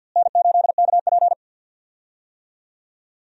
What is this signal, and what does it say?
Morse code, heard as a keyed tone, about 52 wpm, 700 Hz. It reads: N8KP
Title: USA Call Signs - 4 Characters - Repeated 2x - 50wpm